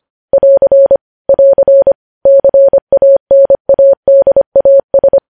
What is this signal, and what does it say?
Morse code, heard as a keyed tone, about 25 wpm, 550 Hz. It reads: <AR> <AR> CANADAS
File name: podcast / News headlines-1x-25x25